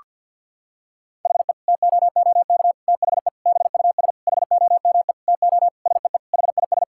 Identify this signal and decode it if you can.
Morse code, heard as a keyed tone, about 50 wpm, 700 Hz. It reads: HE TOOK THE BUS HOME TO SEE HIS
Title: Sentences from Top 1000 Words - Review Part 6 - 50wpm